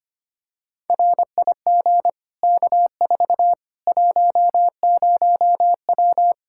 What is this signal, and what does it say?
Morse code, heard as a keyed tone, about 25 wpm, 700 Hz. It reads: RIG K4 10W
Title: K1USN SST QSO Practice 25wpm